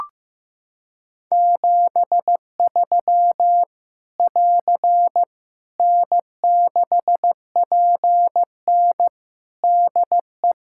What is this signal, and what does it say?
Morse code, heard as a keyed tone, about 16 wpm, 700 Hz. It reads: 73 <AR> N6PN DE
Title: Practice QSO - Exchange 4 - Ending - 15wpm